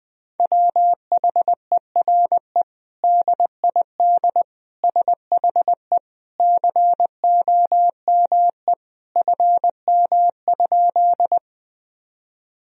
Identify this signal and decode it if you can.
Morse code, heard as a keyed tone, about 20 wpm, 700 Hz. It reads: WHERE DID SHE COME FM?
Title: Sentences from Top 200 Words and Abbreviations 20wpm